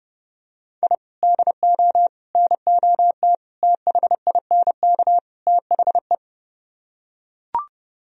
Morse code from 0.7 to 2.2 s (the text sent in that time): I DO